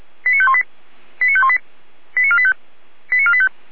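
Five-tone selective call (ZVEI), heard as quick runs of five tones, tones of about 70 ms. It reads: 89518 89518 89485 89485
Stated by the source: ZVEI five-tone selective call 89436, received on an SDR.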